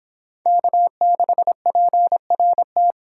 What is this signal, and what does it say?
Morse code, heard as a keyed tone, about 25 wpm, 700 Hz. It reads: K6PRT